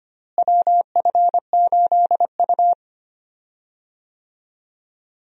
Morse code, heard as a keyed tone, about 25 wpm, 700 Hz. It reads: WF8U
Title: USA Call Signs - 4 Characters - Repeated 2x - 25wpm